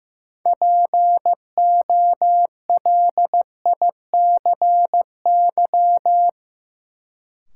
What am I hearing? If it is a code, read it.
Morse code, heard as a keyed tone, about 15 wpm, 700 Hz. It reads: POLICY
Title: Sentences from Top 1000 Words - Review Part 5 - New Word Drill - 15wpm